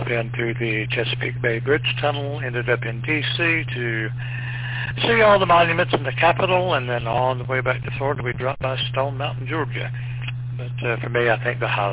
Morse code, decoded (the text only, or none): none